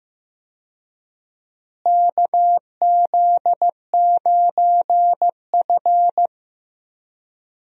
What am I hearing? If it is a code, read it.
Morse code, heard as a keyed tone, about 15 wpm, 700 Hz. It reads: KZ9F
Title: USA Call Signs - 4 Characters - Repeated 2x - 15wpm